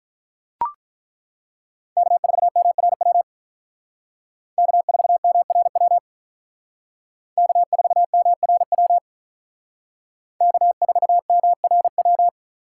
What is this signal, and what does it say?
Morse code, heard as a keyed tone, about 42 wpm, 700 Hz. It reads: K4MRW K4MRW K4MRW K4MRW
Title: Speed Racing - USA Call Signs 35wpm